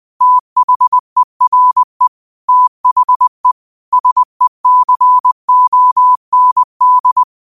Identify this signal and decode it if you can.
Morse code, heard as a keyed tone, about 20 wpm, 1000 Hz. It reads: THERE THE SECOND
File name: qotdfiles / 20wpm